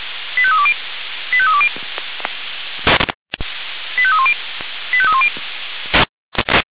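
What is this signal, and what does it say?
Five-tone selective call (ZVEI), heard as quick runs of five tones, tones of about 70 ms. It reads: 85320 85320 85310 85310
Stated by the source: ZVEI five-tone selective call received by SDR, address 85320.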